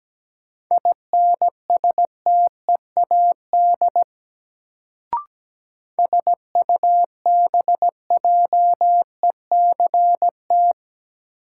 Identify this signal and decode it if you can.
Morse code, heard as a keyed tone, about 17 wpm, 700 Hz. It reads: INSTEAD SUBJECT
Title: Sentences from Top 800 Words - New Word Drill - 17wpm